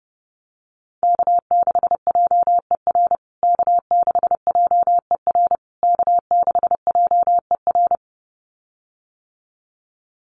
Morse code, heard as a keyed tone, about 30 wpm, 700 Hz. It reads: K6JER K6JER K6JER
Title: USA Call Signs Repeated 3x 30wpm